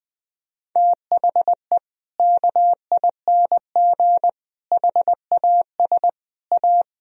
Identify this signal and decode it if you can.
Morse code, heard as a keyed tone, about 20 wpm, 700 Hz. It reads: THE KING HAS A
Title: Sentences from Top 1000 Words - Review Part 4 - 20wpm